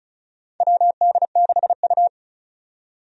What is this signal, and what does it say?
Morse code, heard as a keyed tone, about 36 wpm, 700 Hz. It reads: WD6U
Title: USA Call Signs - 4 Characters - 35wpm